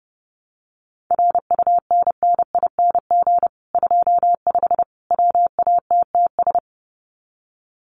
Morse code, heard as a keyed tone, about 30 wpm, 700 Hz. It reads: RUNNING 25 WATTS
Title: Common QSO Phrases 30wpm